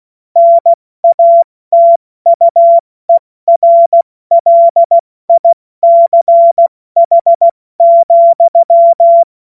Morse code, heard as a keyed tone, about 16 wpm, 680 Hz. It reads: NATUERLICH,
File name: Morsekurs / Witze / Witz 21. September 2021, 70 F80